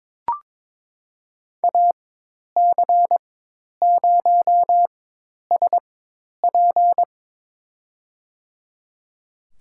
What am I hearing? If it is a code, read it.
Morse code, heard as a keyed tone, about 22 wpm, spaced as for 10 wpm, 700 Hz. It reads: AC0SP
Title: USA Call Signs - 4x Character Spacing - 22wpm